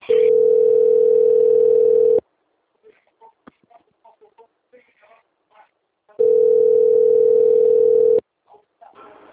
Ringback tone (North America), 440 + 480 Hz.